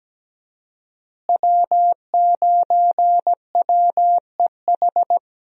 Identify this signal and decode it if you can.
Morse code, heard as a keyed tone, about 17 wpm, 700 Hz. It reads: W9WEH
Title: USA Call Signs - 5 Characters - Repeated 2x - 17wpm